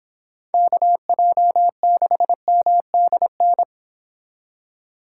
Morse code, heard as a keyed tone, about 26 wpm, 700 Hz. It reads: KJ6MDN